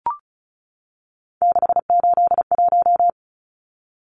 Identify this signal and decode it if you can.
Morse code, heard as a keyed tone, about 34 wpm, 700 Hz. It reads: 681
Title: Numbers - 3-Digit 35wpm